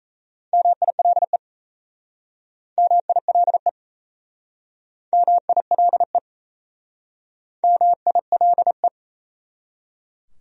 Morse code, heard as a keyed tone, about 39 wpm, 700 Hz. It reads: MILE MILE MILE MILE